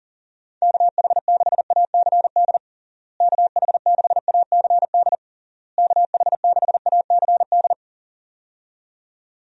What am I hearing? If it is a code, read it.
Morse code, heard as a keyed tone, about 40 wpm, 700 Hz. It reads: KH6ACD KH6ACD KH6ACD
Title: USA Call Signs - 6 Characters - Repeated 3x - 40wpm